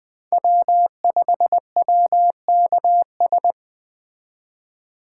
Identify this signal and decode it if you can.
Morse code, heard as a keyed tone, about 19 wpm, 700 Hz. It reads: W5WKS